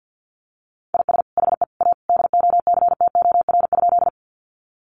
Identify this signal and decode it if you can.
Morse code, heard as a keyed tone, about 50 wpm, 700 Hz. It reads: IS HE A DOCTOR?
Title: Sentences from Top 1500 Words 50wpm